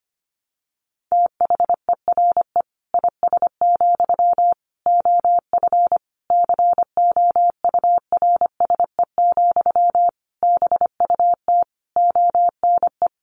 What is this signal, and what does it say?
Morse code, heard as a keyed tone, about 25 wpm, 700 Hz. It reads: THERE IS, OF COURSE, BUT ONE
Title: Sentences from Top 1200 Words 25wpm